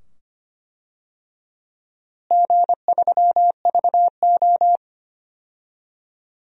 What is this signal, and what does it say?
Morse code, heard as a keyed tone, about 25 wpm, 700 Hz. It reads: G3VO
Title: DXCC-100 Call Signs - 4 Characters - Repeated 2x - 25wpm